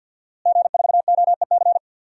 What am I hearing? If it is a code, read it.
Morse code, heard as a keyed tone, about 50 wpm, 700 Hz. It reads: G4OEC
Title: DXCC-100 Call Signs - 5 Characters - Repeated 2x - 50wpm